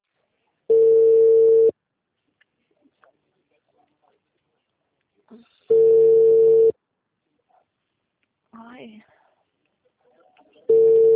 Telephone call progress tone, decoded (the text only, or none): ringback tone